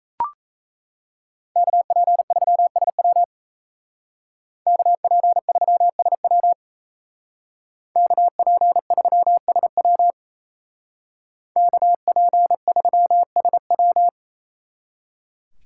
Morse code, heard as a keyed tone, about 37 wpm, 700 Hz. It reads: KP3SW KP3SW KP3SW KP3SW